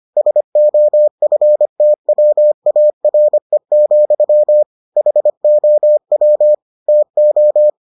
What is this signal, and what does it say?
Morse code, heard as a keyed tone, about 25 wpm, 600 Hz. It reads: SOFTWARE, HOW TO